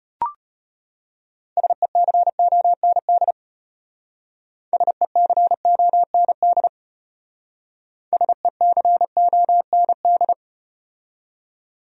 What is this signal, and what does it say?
Morse code, heard as a keyed tone, about 33 wpm, 700 Hz. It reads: SECOND SECOND SECOND